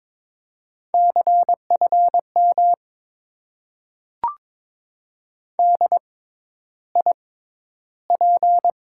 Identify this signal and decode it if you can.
Morse code, heard as a keyed tone, about 21 wpm, 700 Hz. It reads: C F M D I P
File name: Top 90 QSO Elements - 6x Character Spacing - 22wpm